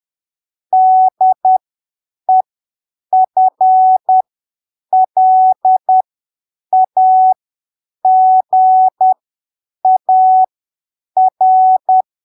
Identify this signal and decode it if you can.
Morse code, heard as a keyed tone, about 10 wpm, spaced as for 7 wpm, 750 Hz. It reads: DEFLAGAR
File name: arquivos / texto / texto 30 10ppm